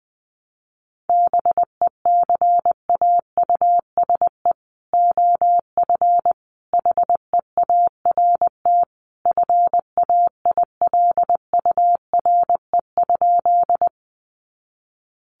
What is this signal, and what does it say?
Morse code, heard as a keyed tone, about 19 wpm, 700 Hz. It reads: BECAUSE OF HEART FAILURE?
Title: Sentences from Top 1200 Words 20wpm